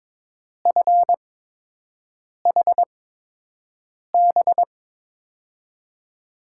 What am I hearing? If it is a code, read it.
Morse code, heard as a keyed tone, about 22 wpm, spaced as for 6 wpm, 700 Hz. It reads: FHB
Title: USA Call Signs - 8x Character Spacing - 22wpm